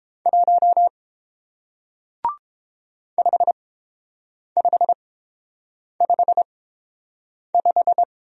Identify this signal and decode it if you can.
Morse code, heard as a keyed tone, about 29 wpm, spaced as for 7 wpm, 700 Hz. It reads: 1 5555